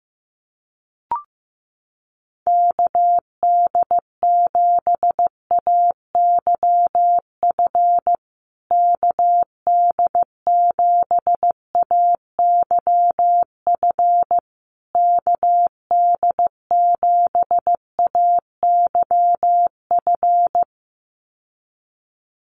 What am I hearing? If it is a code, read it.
Morse code, heard as a keyed tone, about 15 wpm, 700 Hz. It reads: KD7AYF KD7AYF KD7AYF
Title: USA Call Signs Repeated 3x 15wpm